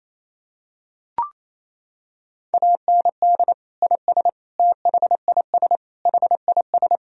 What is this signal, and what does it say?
Morse code, heard as a keyed tone, about 28 wpm, 700 Hz. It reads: AND IS THIS HIS
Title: Sentences from Top 1000 Words - Review Part 1 - 28wpm